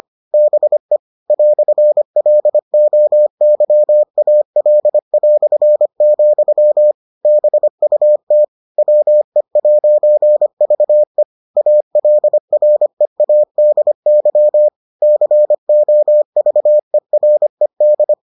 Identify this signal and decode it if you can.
Morse code, heard as a keyed tone, about 25 wpm, 600 Hz. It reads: BE "LOYAL", BUT WE'VE ALREADY COVERED